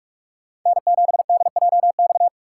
Morse code, heard as a keyed tone, about 44 wpm, 700 Hz. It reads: N7DJX
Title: USA Call Signs - 5 Characters - 6x Character Spacing - 45wpm